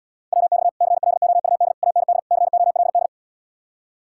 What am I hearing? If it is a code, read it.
Morse code, heard as a keyed tone, about 77 wpm, 700 Hz. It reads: WX CLOUD AND COLD